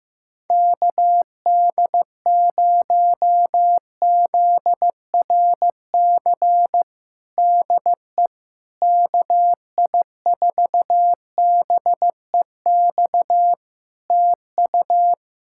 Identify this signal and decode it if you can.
Morse code, heard as a keyed tone, about 15 wpm, 700 Hz. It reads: KD0ZRC DE KI4BEX TU
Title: Practice QSO - Exchange 4 - Ending - 15wpm